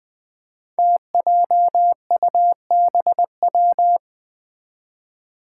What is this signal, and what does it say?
Morse code, heard as a keyed tone, about 20 wpm, 700 Hz. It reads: TJUBW
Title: Five Challenging Letter-Number Combinations 20wpm